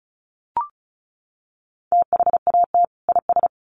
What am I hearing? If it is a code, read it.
Morse code, heard as a keyed tone, about 34 wpm, 700 Hz. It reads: THAT IS
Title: Sets of 2 Words 35wpm